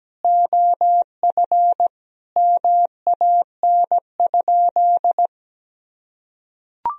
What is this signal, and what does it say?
Morse code, heard as a keyed tone, about 17 wpm, 700 Hz. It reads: OF MAN?
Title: Sentences from Top 1000 Words - Review Part 4 - 17wpm